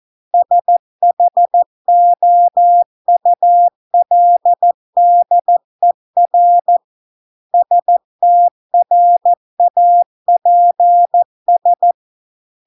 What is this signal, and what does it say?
Morse code, heard as a keyed tone, about 14 wpm, 700 Hz. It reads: SHOULDER STRAPS